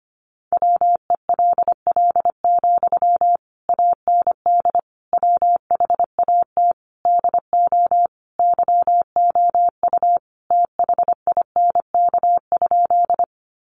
Morse code, heard as a keyed tone, about 25 wpm, 700 Hz. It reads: WELL, AND WHAT DO YOU THINK?
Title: Sentences from Top 200 Words 25wpm